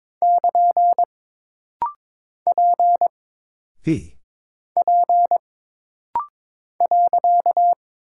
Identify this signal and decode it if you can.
Morse code, heard as a keyed tone, about 22 wpm, 700 Hz. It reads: <KN> P P .
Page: This rapid-fire timing is geared to push your ICR (Instant Character Recognition) skills.